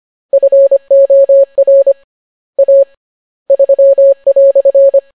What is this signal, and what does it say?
Morse code, heard as a keyed tone, about 25 wpm, 550 Hz. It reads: FOR A 3"